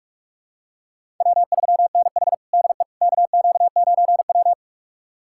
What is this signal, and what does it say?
Morse code, heard as a keyed tone, about 46 wpm, 700 Hz. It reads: W3NH DE KQ9W
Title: USA Callsign DE Callsign - 4 Characters - 45wpm